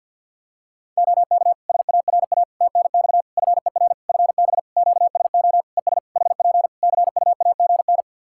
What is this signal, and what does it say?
Morse code, heard as a keyed tone, about 50 wpm, 700 Hz. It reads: OK SARA TNX FER FB QSO ES HP CUAGN
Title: Practice QSO - Exchange 4 - Ending - 50wpm